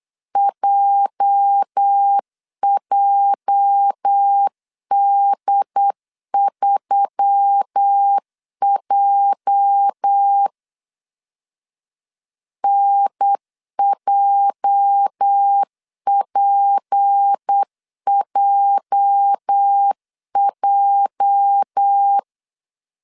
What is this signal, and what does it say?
Morse code, heard as a keyed tone, about 8 wpm, 800 Hz. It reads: JJD3J NJPJJ
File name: LEZIONI / 8 YJ45 / 8 YJ 8wpm 3313